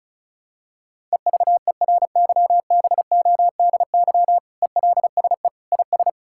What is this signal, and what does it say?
Morse code, heard as a keyed tone, about 34 wpm, 700 Hz. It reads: EVERYBODY ELSE IS
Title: Sentences from Top 1600 Words 35wpm